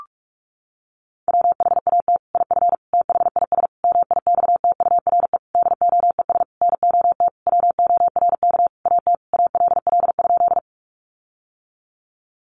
Morse code, heard as a keyed tone, about 46 wpm, 700 Hz. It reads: WHAT IF THIS MIXTURE DOES NOT WORK AT ALL?
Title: Sentences from Top 1500 Words 45wpm